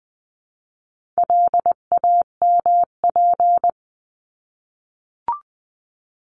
Morse code, heard as a keyed tone, about 20 wpm, 700 Hz. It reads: LAMP
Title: 259 Lesson 42d - Words 18-20wpm